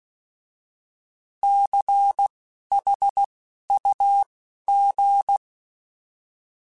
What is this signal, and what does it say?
Morse code, heard as a keyed tone, about 16 wpm, spaced as for 11 wpm, 800 Hz. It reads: CHUG
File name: Alphabetical / Alphabetical mp3s / Cs @10